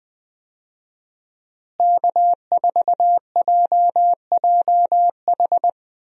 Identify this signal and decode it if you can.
Morse code, heard as a keyed tone, about 20 wpm, 700 Hz. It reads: K4JJH
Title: USA Call Signs - 5 Characters - Repeated 2x - 20wpm